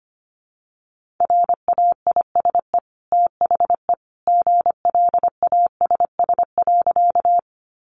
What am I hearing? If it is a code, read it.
Morse code, heard as a keyed tone, about 25 wpm, 700 Hz. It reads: RAISE THE GLASS.
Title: Sentences from Top 1400 Words 25wpm